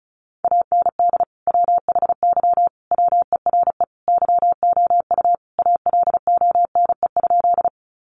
Morse code, heard as a keyed tone, about 34 wpm, 700 Hz. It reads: AND WHY WERE YOU ALONE?
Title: Sentences from Top 1000 Words 35wpm